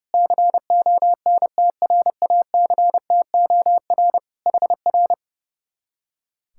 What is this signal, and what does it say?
Morse code, heard as a keyed tone, about 28 wpm, 700 Hz. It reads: CONTRACTOR HR
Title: Speed-Racing Common QSO Phrases Encore - 30wpm